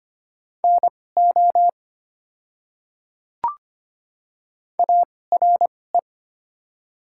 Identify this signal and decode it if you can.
Morse code, heard as a keyed tone, about 25 wpm, spaced as for 18 wpm, 700 Hz. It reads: NO ARE
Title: 2x Character Spacing - Top 100 Words - 25wpm